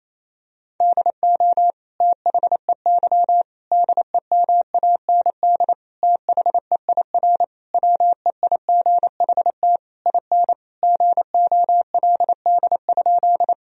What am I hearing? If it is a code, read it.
Morse code, heard as a keyed tone, about 28 wpm, 700 Hz. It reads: DO THEY DEMAND THEIR WEIGHT IN GOLD?
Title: Sentences from Top 1000 Words - Review Part 3 - 28wpm